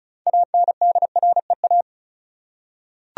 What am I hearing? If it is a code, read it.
Morse code, heard as a keyed tone, about 36 wpm, 700 Hz. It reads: ANDREA
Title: USA Common Names - 35wpm